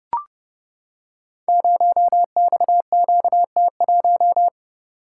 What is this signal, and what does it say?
Morse code, heard as a keyed tone, about 30 wpm, 700 Hz. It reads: 0XQT1
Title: Code Groups - 5 Letter:Numbers - 30wpm